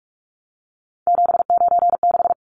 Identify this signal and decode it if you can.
Morse code, heard as a keyed tone, about 44 wpm, 700 Hz. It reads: 796
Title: Numbers - 3-Digit 45wpm